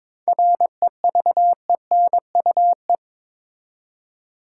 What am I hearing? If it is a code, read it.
Morse code, heard as a keyed tone, about 23 wpm, 700 Hz. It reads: REVENUE
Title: Seven Letter Words - 22wpm